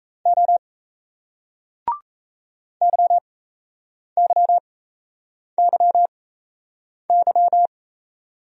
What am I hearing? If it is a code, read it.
Morse code, heard as a keyed tone, about 37 wpm, spaced as for 8 wpm, 700 Hz. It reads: O YYYY